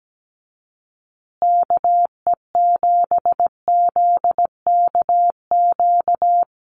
Morse code, heard as a keyed tone, about 17 wpm, 700 Hz. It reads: KE7ZKQ
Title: USA Call Signs Repeated 2x 17wpm